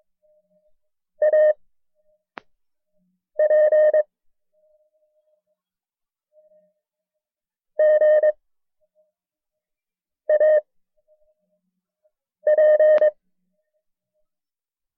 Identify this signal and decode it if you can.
Morse code, heard as a keyed tone, about 22 wpm, 610 Hz. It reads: A P G A P
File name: FreeFiles / CW-practice / KB6MT-tape1-side2